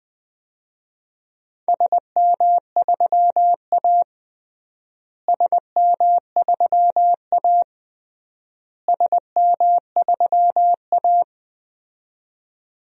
Morse code, heard as a keyed tone, about 21 wpm, 700 Hz. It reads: SM3A SM3A SM3A
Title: DXCC-100 Call Signs - 4 Characters - Repeated 3x - 20wpm